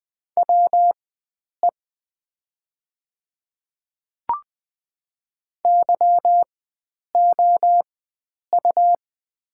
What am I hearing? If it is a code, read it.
Morse code, heard as a keyed tone, about 20 wpm, spaced as for 9 wpm, 700 Hz. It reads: WE YOU
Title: 4x Character Spacing - Top 100 Words - 20wpm